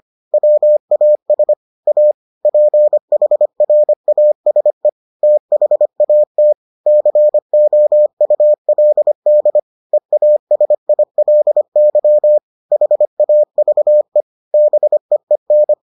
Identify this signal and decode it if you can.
Morse code, heard as a keyed tone, about 25 wpm, 600 Hz. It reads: WAS A PHRASE THAT COULD EASILY HAVE BEEN